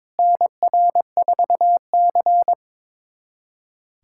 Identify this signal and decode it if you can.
Morse code, heard as a keyed tone, about 21 wpm, 700 Hz. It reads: NR4C
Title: USA Callsign DE Callsign - 4 Characters - 22wpm